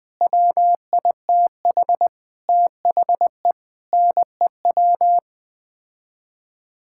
Morse code, heard as a keyed tone, about 21 wpm, 700 Hz. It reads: WITH THE NEW
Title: Sets of 8 Words - 20wpm